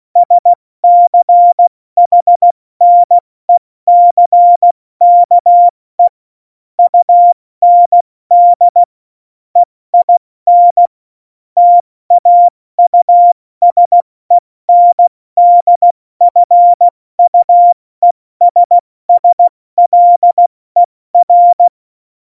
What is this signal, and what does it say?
Morse code, heard as a keyed tone, about 16 wpm, 700 Hz. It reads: SCHNECKE UND EIN TAUSENDFUESSLER
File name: Morsekurs / Witze / Witz 27 March 2018, 70 F80